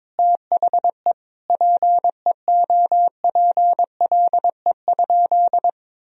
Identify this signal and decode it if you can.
Morse code, heard as a keyed tone, about 22 wpm, 700 Hz. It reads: THE PEOPLE?
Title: Sentences from Top 1000 Words - Review Part 3 - 22wpm